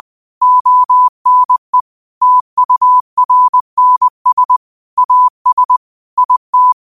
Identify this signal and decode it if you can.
Morse code, heard as a keyed tone, about 20 wpm, 1000 Hz. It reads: ONE TURNS AS IT
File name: qotdfiles / 20wpm